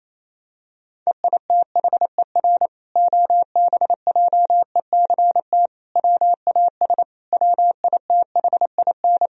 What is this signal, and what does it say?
Morse code, heard as a keyed tone, about 28 wpm, 700 Hz. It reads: EITHER OBJECT WAS WITHIN
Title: Sentences from Top 1000 Words - Review Part 4 - 28wpm